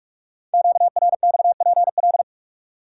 Morse code, heard as a keyed tone, about 44 wpm, 700 Hz. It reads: QRXPL
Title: Code Groups - 5 Letters - 45wpm